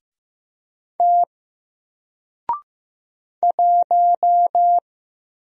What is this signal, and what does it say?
Morse code, heard as a keyed tone, about 15 wpm, 700 Hz. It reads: T 1
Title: Single Character (Letter:Number:Punctuation:Prosign) Alt 15wpm